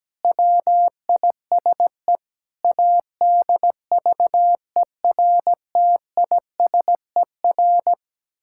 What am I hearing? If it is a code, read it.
Morse code, heard as a keyed tone, about 17 wpm, 700 Hz. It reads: WISE ADVERTISER
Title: Pangram Challenge v2 - 17wpm